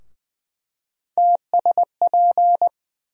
Morse code, heard as a keyed tone, about 20 wpm, 700 Hz. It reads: TSP
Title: USA Call Sign Postfixes - 3 Letters 20wpm